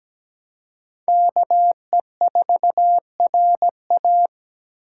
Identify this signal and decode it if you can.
Morse code, heard as a keyed tone, about 17 wpm, 700 Hz. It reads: KE4RA
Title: USA Call Signs - 5 Characters - 2x Character Spacing - 17wpm